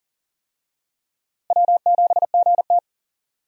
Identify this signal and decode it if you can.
Morse code, heard as a keyed tone, about 40 wpm, 700 Hz. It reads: W7GT
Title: USA Call Signs - 4 Characters - Repeated 3x - 40wpm